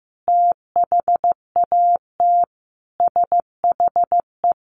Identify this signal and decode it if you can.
Morse code, heard as a keyed tone, about 15 wpm, 700 Hz. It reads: THAT SHE
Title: Sentences from Top 800 Words 15wpm